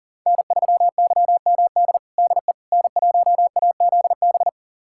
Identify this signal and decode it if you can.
Morse code, heard as a keyed tone, about 37 wpm, 700 Hz. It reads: N3YMD DE N1AZB
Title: USA Callsign DE Callsign - 5 Characters - 40wpm